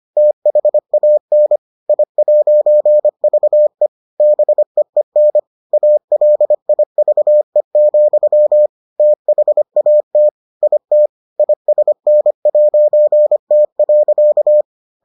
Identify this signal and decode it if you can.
Morse code, heard as a keyed tone, about 25 wpm, 600 Hz. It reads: THAN I'VE BEEN ALIVE, THAT IT ISN'T.